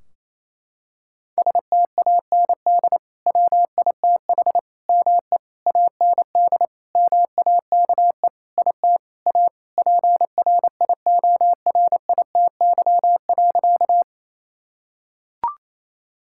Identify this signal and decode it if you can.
Morse code, heard as a keyed tone, about 28 wpm, 700 Hz. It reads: STAND WITH ME AND MAKE IT A PRIORITY.
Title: Sentences from Top 1600 Words 28wpm